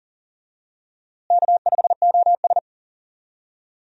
Morse code, heard as a keyed tone, about 40 wpm, 700 Hz. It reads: K5OS
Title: USA Call Signs - 4 Characters - Repeated 3x - 40wpm